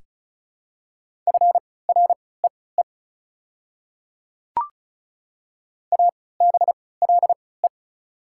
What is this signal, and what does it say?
Morse code, heard as a keyed tone, about 36 wpm, spaced as for 20 wpm, 690 Hz. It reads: FREE ABLE
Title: 3x Character Spacing - Top 500 Words - 35wpm